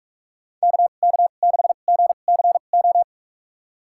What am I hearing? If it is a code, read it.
Morse code, heard as a keyed tone, about 46 wpm, spaced as for 32 wpm, 700 Hz. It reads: KK6GCO
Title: USA Call Signs - 6 Characters - 2x Character Spacing - 45wpm